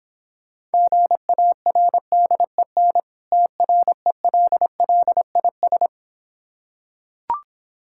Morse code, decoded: GARDEN TRELLIS

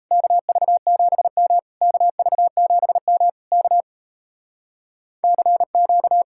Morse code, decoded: KV7M KV7M K CQ

